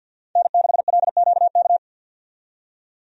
Morse code, heard as a keyed tone, about 52 wpm, 700 Hz. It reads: N6LQK
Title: USA Call Signs - 5 Characters - 50wpm